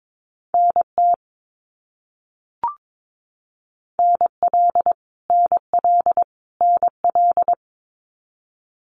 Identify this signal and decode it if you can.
Morse code, heard as a keyed tone, about 22 wpm, 700 Hz. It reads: NT NL NL NL
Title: Canadian Provinces and Territories - Repeated 3x - 22wpm